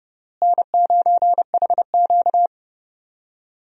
Morse code, heard as a keyed tone, about 30 wpm, 700 Hz. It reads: N9HQ
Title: USA Call Signs - 4 Characters - Repeated 3x - 30wpm